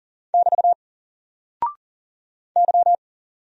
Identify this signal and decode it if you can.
Morse code, heard as a keyed tone, about 40 wpm, 700 Hz. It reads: <BT> Y
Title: Single Character (Letter:Number:Punctuation:Prosign) 40wpm